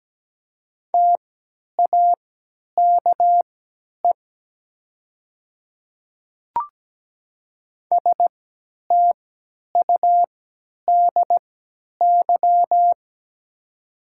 Morse code, heard as a keyed tone, about 17 wpm, spaced as for 10 wpm, 700 Hz. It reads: TAKE STUDY
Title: 3x Character Spacing - Top 300 Words - 17wpm